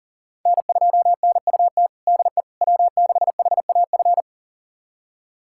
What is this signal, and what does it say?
Morse code, heard as a keyed tone, about 40 wpm, 700 Hz. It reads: N2NUT DE W6HAF
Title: USA Callsign DE Callsign - 5 Characters - 40wpm